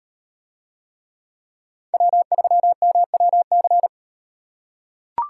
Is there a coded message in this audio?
Yes: Morse code from 1.9 to 3.9 s, heard as a keyed tone, about 38 wpm, 700 Hz. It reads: W3MWC